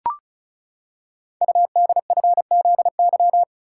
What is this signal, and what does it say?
Morse code, heard as a keyed tone, about 35 wpm, 690 Hz. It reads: UDFZY
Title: Code Groups - 5 Letters - 35wpm